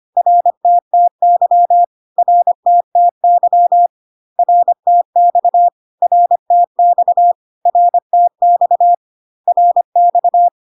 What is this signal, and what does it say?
Morse code, heard as a keyed tone, about 25 wpm, 700 Hz. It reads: RTTY RTTY RTX RTX RTX RX
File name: cw / CW common abbreviations alphabetical repeated 3 times 25 25wpm ik1hge